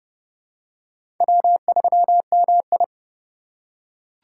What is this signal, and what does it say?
Morse code, heard as a keyed tone, about 30 wpm, 700 Hz. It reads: W3MI
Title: USA Call Signs - 4 Characters - 3x Character Spacing - 30wpm